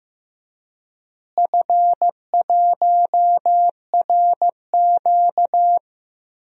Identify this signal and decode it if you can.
Morse code, heard as a keyed tone, about 15 wpm, 700 Hz. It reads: F1RQ
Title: DXCC-100 Call Signs - 4 Characters - Repeated 2x - 15wpm